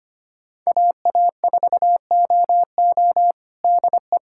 Morse code, heard as a keyed tone, about 25 wpm, 700 Hz. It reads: AA4OO DE
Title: USA Callsign DE Callsign - 5 Characters - 25wpm